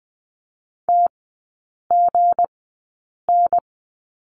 Morse code, heard as a keyed tone, about 20 wpm, spaced as for 8 wpm, 700 Hz. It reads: TGN
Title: 172 Lesson 28e - Number 9 - Three Characters 20wpm